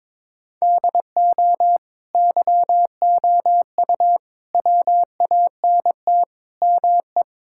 Morse code, heard as a keyed tone, about 22 wpm, 700 Hz. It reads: DO YOU WANT ME
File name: practice / 22wpm / Sentences with 4-Word Phrases - 22wpm